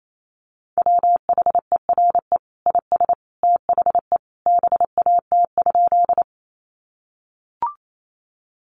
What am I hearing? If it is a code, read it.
Morse code, heard as a keyed tone, about 28 wpm, 700 Hz. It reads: WHERE IS THE BAT?
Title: Sentences from Top 1500 Words 28wpm